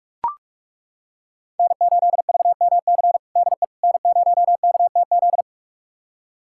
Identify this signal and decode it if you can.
Morse code, heard as a keyed tone, about 41 wpm, 700 Hz. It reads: N8VMC DE N0KTZ
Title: USA Callsign DE Callsign - 5 Characters - 45wpm